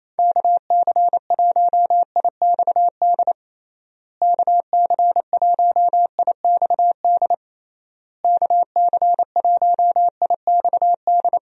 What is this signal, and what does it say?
Morse code, heard as a keyed tone, about 27 wpm, 700 Hz. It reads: KC1IXD KC1IXD KC1IXD
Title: USA Call Signs - 6 Characters - Repeated 3x - 28wpm